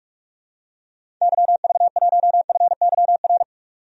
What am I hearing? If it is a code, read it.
Morse code, heard as a keyed tone, about 41 wpm, 700 Hz. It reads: YV1FYR
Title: DXCC-100 Call Signs - 6 Characters - 45wpm